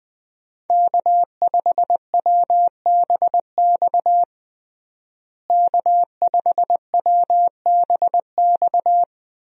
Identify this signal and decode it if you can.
Morse code, heard as a keyed tone, about 20 wpm, 700 Hz. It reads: K5WBX K5WBX
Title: USA Call Signs - 5 Characters - Repeated 2x - 20wpm